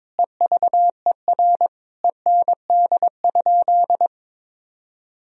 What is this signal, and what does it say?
Morse code, heard as a keyed tone, about 22 wpm, 700 Hz. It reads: EVER END?
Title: Sentences from Top 1000 Words - Review Part 4 - 22wpm